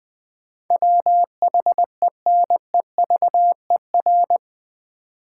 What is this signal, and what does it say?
Morse code, heard as a keyed tone, about 21 wpm, 700 Hz. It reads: WHENEVER
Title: Sentences from Top 1200 Words - New Word Drill - 20wpm